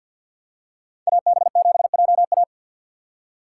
Morse code, heard as a keyed tone, about 53 wpm, 700 Hz. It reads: AB7JA
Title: USA Call Signs - 5 Characters - 3x Character Spacing - 50wpm